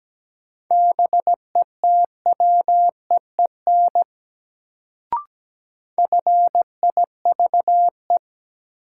Morse code, heard as a keyed tone, about 17 wpm, 700 Hz. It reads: BETWEEN FIVE
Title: Sentences from Top 400 Words - New Word Drill - 17wpm